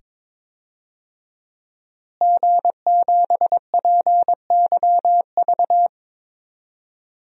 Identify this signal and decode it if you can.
Morse code, heard as a keyed tone, about 22 wpm, 700 Hz. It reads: G7PYV
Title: DXCC-100 Call Signs - 5 Characters - Repeated 2x - 22wpm